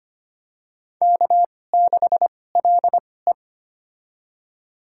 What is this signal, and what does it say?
Morse code, heard as a keyed tone, about 25 wpm, spaced as for 18 wpm, 700 Hz. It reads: K6LE
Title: USA Call Signs - 4 Characters - 2x Character Spacing - 25wpm